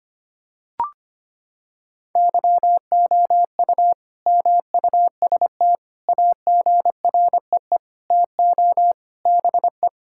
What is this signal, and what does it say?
Morse code, heard as a keyed tone, about 25 wpm, 700 Hz. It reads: YOU MUST AGREE TO BE
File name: practice / 25wpm / Sentences with 9-Word Phrases - 25wpm